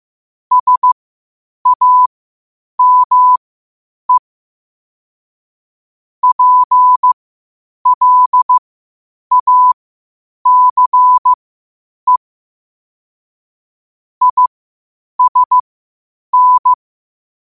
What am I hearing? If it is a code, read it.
Morse code, heard as a keyed tone, about 15 wpm, spaced as for 8 wpm, 1000 Hz. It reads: SAME PLACE ISN
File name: qotdfiles / 8wpm